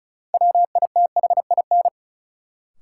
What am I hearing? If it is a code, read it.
Morse code, heard as a keyed tone, about 36 wpm, 700 Hz. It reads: WITHIN
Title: Sets of 2 Words - New Word Drill - 35wpm